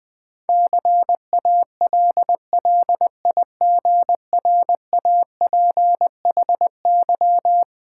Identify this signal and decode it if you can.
Morse code, heard as a keyed tone, about 21 wpm, 700 Hz. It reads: CALLIGRAPHY
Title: Difficult Words with Hints - 20wpm